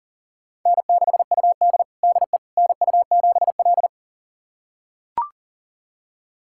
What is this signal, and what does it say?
Morse code, heard as a keyed tone, about 40 wpm, 700 Hz. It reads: N6UD DE NU7L
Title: USA Callsign DE Callsign - 4 Characters - 40wpm